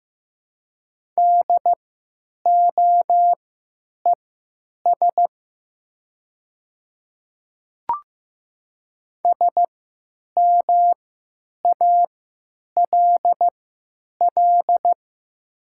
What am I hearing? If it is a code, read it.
Morse code, heard as a keyed tone, about 15 wpm, spaced as for 9 wpm, 700 Hz. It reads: DOES SMALL
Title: 3x Character Spacing - Top 300 Words - 15wpm